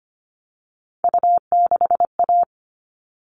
Morse code, heard as a keyed tone, about 25 wpm, 700 Hz. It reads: U6A
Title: All Cut Numbers - 3-Digit 25wpm